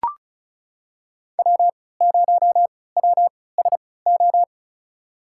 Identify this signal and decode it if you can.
Morse code, heard as a keyed tone, about 36 wpm, spaced as for 20 wpm, 700 Hz. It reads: W0WSO
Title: USA Call Signs - 5 Characters - 3x Character Spacing - 35wpm